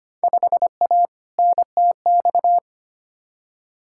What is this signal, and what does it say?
Morse code, heard as a keyed tone, about 25 wpm, 700 Hz. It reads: 5A NTX
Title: Field Day Exchanges - 25wpm